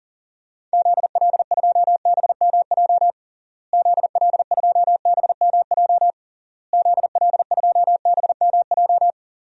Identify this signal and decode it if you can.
Morse code, heard as a keyed tone, about 40 wpm, 700 Hz. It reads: ZL2BMJ ZL2BMJ ZL2BMJ
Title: DXCC-100 Call Signs - 6 Characters - Repeated 3x - 40wpm